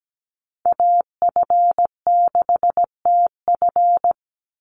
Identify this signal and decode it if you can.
Morse code, heard as a keyed tone, about 17 wpm, 700 Hz. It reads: AF6TF
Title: USA Call Signs 17wpm